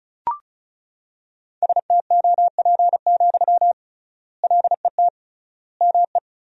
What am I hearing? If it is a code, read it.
Morse code, heard as a keyed tone, about 36 wpm, 700 Hz. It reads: STOP, LET ME
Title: 3x Word-Spacing - Sentences from Top 300 Words - 35wpm